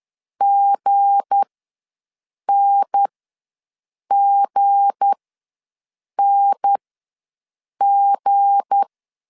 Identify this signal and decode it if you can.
Morse code, heard as a keyed tone, about 10 wpm, spaced as for 6 wpm, 800 Hz. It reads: GNGNG
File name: LEZIONI / 7 GF9Z / 7 G ws 10wpm 3843